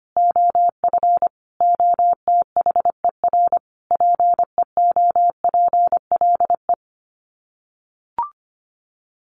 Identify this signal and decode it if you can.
Morse code, heard as a keyed tone, about 25 wpm, 700 Hz. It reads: OF OTHER PEOPLE
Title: Sets of 5 Words Encore2 25wpm